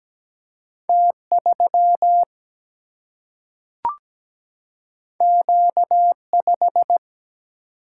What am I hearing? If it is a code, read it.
Morse code, heard as a keyed tone, about 17 wpm, 700 Hz. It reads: T3 Q5
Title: Single-Letter Single-Number 17wpm